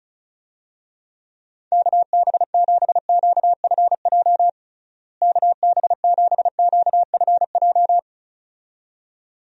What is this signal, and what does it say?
Morse code, heard as a keyed tone, about 34 wpm, 700 Hz. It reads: KB7QFJ KB7QFJ
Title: USA Call Signs - 6 Characters - Repeated 2x - 35wpm